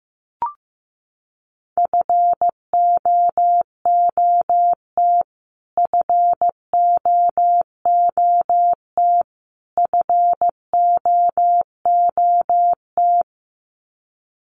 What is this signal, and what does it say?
Morse code, heard as a keyed tone, about 15 wpm, 700 Hz. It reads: FOOT FOOT FOOT
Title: Top 500 Words Repeated 15wpm